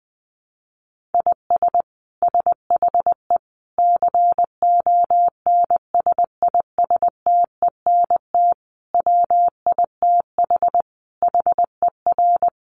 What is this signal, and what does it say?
Morse code, heard as a keyed tone, about 20 wpm, 700 Hz. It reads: IS SHE CONSISTENT WITH HER
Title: Sentences from Top 1500 Words 20wpm